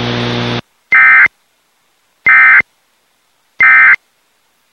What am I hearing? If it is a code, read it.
Emergency Alert System (SAME) tones, heard as harsh buzzing bursts. It is heard from 0.9 to 3.9 s.